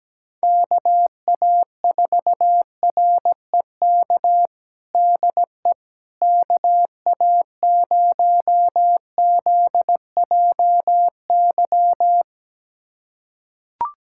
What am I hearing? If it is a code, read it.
Morse code, heard as a keyed tone, about 17 wpm, 700 Hz. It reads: KA4REK DE KA0ZJY
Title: USA Callsign DE Callsign 17wpm